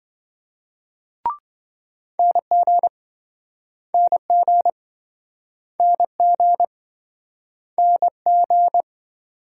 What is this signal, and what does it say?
Morse code, heard as a keyed tone, about 27 wpm, 700 Hz. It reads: NG NG NG NG